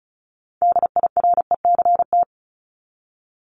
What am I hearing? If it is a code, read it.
Morse code, heard as a keyed tone, about 34 wpm, 700 Hz. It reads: DIRECT
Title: Top 500 Words 35wpm